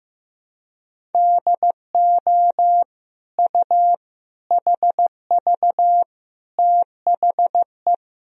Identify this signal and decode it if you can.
Morse code, heard as a keyed tone, about 15 wpm, 700 Hz. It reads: DO U HV THE
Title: Sentences from Top 300 Words and Abbreviations 15wpm